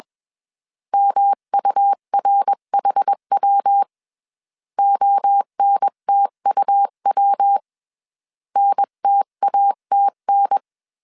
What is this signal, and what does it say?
Morse code, heard as a keyed tone, about 21 wpm, 800 Hz. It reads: MURHW ONTUW NTATN
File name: 20wpm